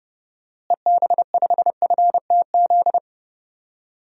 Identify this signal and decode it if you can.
Morse code, heard as a keyed tone, about 30 wpm, 700 Hz. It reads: EB5FTZ
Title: DXCC-100 Call Signs - 6 Characters - Repeated 2x - 30wpm